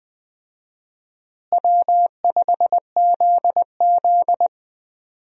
Morse code, heard as a keyed tone, about 20 wpm, 700 Hz. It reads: W5ZZ
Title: USA Call Signs - 4 Characters - Repeated 3x - 20wpm